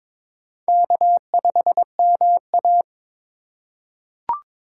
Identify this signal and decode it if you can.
Morse code, heard as a keyed tone, about 22 wpm, 700 Hz. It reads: K5MA
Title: K1USN SST QSO Practice - 22wpm